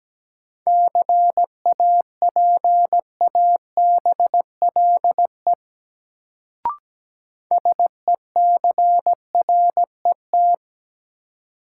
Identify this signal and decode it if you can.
Morse code, heard as a keyed tone, about 17 wpm, 700 Hz. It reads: CAPABLE SECRET
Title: Sentences from Top 1200 Words - New Word Drill - 17wpm